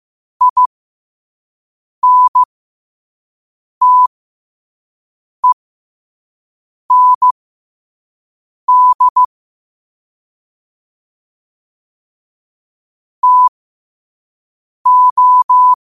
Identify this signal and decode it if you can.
Morse code, heard as a keyed tone, about 15 wpm, spaced as for 5 wpm, 1000 Hz. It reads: INTEND TO